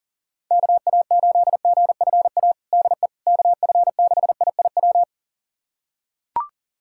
Morse code, heard as a keyed tone, about 40 wpm, 700 Hz. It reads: KA8GFA DE KF6IIW
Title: USA Callsign DE Callsign 40wpm